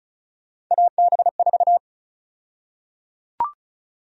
Morse code, heard as a keyed tone, about 36 wpm, 700 Hz. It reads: AB4
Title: USA Call Sign Prefixes - Repeated 2x - 35wpm